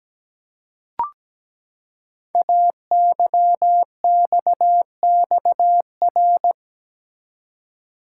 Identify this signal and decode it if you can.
Morse code, heard as a keyed tone, about 17 wpm, 700 Hz. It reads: AYXXR
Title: Code Groups - 5 Letters - 17wpm